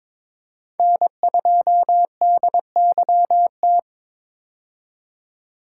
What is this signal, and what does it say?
Morse code, heard as a keyed tone, about 21 wpm, 700 Hz. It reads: N2DYT
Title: K1USN SST QSO Practice - 22wpm